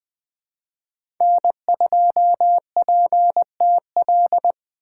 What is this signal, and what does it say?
Morse code, heard as a keyed tone, about 21 wpm, 700 Hz. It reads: N2PTL
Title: USA Call Signs - 4x Character Spacing - 20wpm